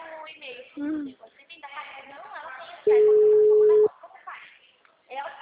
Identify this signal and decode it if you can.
Ringback tone (Europe, 425 Hz), with speech.
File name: out- -2005- - - .403